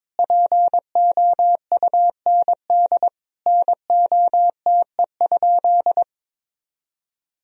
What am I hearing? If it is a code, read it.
Morse code, heard as a keyed tone, about 22 wpm, 700 Hz. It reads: POUND NOTE?
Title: Sentences from Top 1000 Words - Review Part 2 - 22wpm